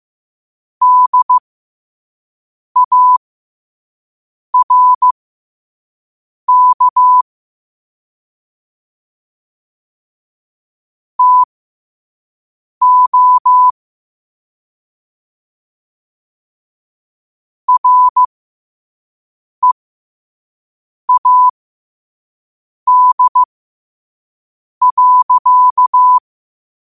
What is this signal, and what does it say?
Morse code, heard as a keyed tone, about 15 wpm, spaced as for 5 wpm, 1000 Hz. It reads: DARK TO READ.
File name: qotdfiles / 5wpm